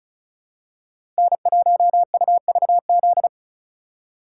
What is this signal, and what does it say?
Morse code, heard as a keyed tone, about 34 wpm, 700 Hz. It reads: N1UVZ
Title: USA Call Signs - 5 Characters - 2x Character Spacing - 35wpm